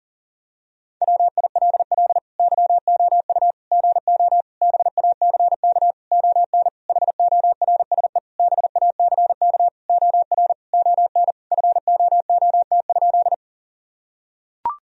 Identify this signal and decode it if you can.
Morse code, heard as a keyed tone, about 40 wpm, 700 Hz. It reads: WILL YOU GO BACK ON HORSE BACK OR ON FOOT?
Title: Sentences from Top 1000 Words - Review Part 3 - 40wpm